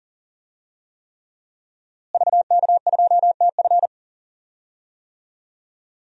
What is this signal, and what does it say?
Morse code, heard as a keyed tone, about 40 wpm, 700 Hz. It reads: VK2TF
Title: DXCC-100 Call Signs - 5 Characters - Repeated 3x - 40wpm